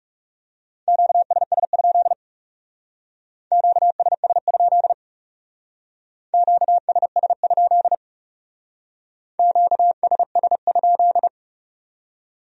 Morse code, heard as a keyed tone, about 36 wpm, 700 Hz. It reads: QSS? QSS? QSS? QSS?
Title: Speed-Racing Q-Codes - 30wpm